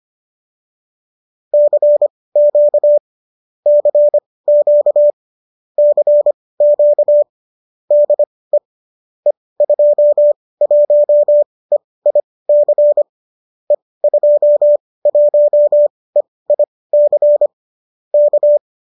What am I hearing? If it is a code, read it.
Morse code, heard as a keyed tone, about 25 wpm, spaced as for 18 wpm, 570 Hz. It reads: CQ CQ CQ DE E21EIC E21EIC K